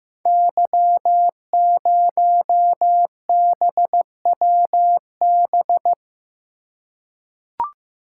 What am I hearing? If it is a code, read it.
Morse code, heard as a keyed tone, about 15 wpm, 700 Hz. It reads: Y0BWB
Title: Code Groups - 5 Letter:Numbers - 15wpm